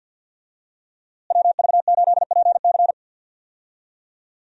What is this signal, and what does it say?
Morse code, heard as a keyed tone, about 49 wpm, 700 Hz. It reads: WV8PC
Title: USA Call Signs - 5 Characters - Repeated 3x - 50wpm